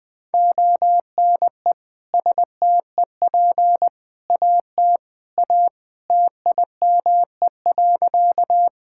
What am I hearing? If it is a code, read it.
Morse code, heard as a keyed tone, about 20 wpm, 700 Hz. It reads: ONE STEP AT A TIME.
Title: Sentences with 3-Word Phrases - 20wpm